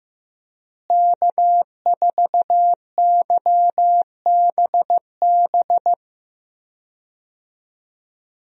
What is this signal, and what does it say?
Morse code, heard as a keyed tone, about 15 wpm, 700 Hz. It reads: K4YBB
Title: USA Call Signs - 5 Characters - Repeated 2x - 15wpm